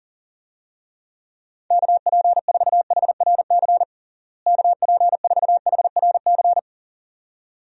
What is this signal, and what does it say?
Morse code, heard as a keyed tone, about 40 wpm, 700 Hz. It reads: KP4HRC KP4HRC
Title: USA Call Signs - 6 Characters - Repeated 2x - 40wpm